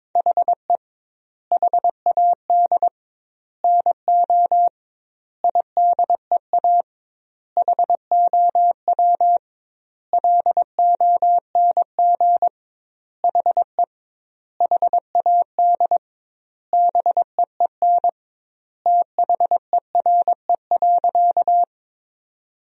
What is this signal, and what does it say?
Morse code, heard as a keyed tone, about 22 wpm, 700 Hz. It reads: HE HAD NO IDEA HOW LONG HE HAD BEEN THERE.
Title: 2x Word-Spacing - Sentences from Top 300 Words - 22wpm